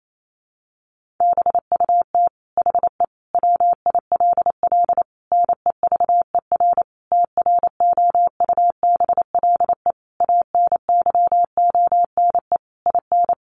Morse code, heard as a keyed tone, about 28 wpm, 700 Hz. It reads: BUT HE WILL NEVER TROUBLE ANYONE IN
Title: Sentences from Top 1200 Words 28wpm